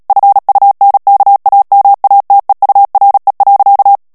Morse code, heard as a keyed tone, about 37 wpm, 800 Hz. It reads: FUNKAMATEURE.
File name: morse mp3 / ALLEMAND texte clair 2 vitesse 35 mots minute